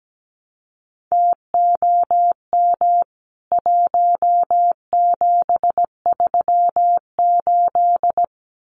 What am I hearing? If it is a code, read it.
Morse code, heard as a keyed tone, about 17 wpm, 700 Hz. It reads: TOM 1738
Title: CWT Exchange 17wpm